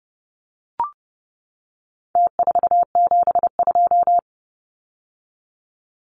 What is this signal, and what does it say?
Morse code, heard as a keyed tone, about 30 wpm, 700 Hz. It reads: T472
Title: Numbers - Contest Style - 4-Digit 30wpm